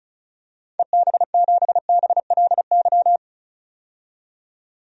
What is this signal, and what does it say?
Morse code, heard as a keyed tone, about 36 wpm, 700 Hz. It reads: EB7BLY
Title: DXCC-100 Call Signs - 6 Characters - 35wpm